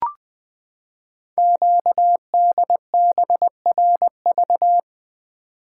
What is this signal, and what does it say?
Morse code, heard as a keyed tone, about 20 wpm, 700 Hz. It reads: QDBRV
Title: Code Groups - 5 Letters - 20wpm